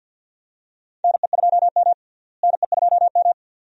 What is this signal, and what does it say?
Morse code, heard as a keyed tone, about 46 wpm, 700 Hz. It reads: NE2M NE2M
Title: USA Call Signs - 4 Characters - Repeated 2x - 50wpm